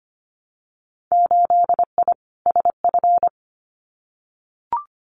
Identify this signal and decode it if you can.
Morse code, heard as a keyed tone, about 25 wpm, 700 Hz. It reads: 8I SF
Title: Winter Field Day Exchanges Encore 25wpm